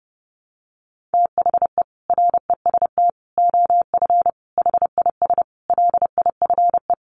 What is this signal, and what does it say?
Morse code, heard as a keyed tone, about 28 wpm, 700 Hz. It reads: THE REST OF HIS LIFE
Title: Sets of 5 Words 30wpm